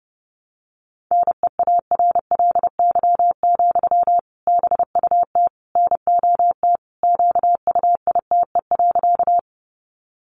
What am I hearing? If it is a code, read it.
Morse code, heard as a keyed tone, about 30 wpm, 700 Hz. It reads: NEARLY, BUT NOT QUITE.
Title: Sentences from Top 1300 Words 30wpm